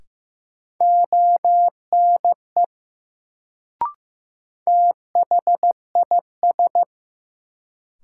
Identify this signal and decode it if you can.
Morse code, heard as a keyed tone, about 15 wpm, 700 Hz. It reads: ONE THIS
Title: Sentences from Top 100 Words - New Word Drill - 15wpm